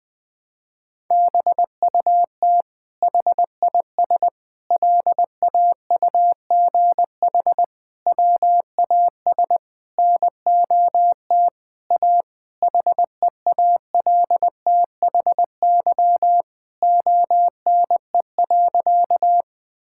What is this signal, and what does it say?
Morse code, heard as a keyed tone, about 20 wpm, 700 Hz. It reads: BUT HIS LAUGH WAS NOT A HEALTHY ONE.
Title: Sentences from Top 1000 Words - Review Part 5 - 20wpm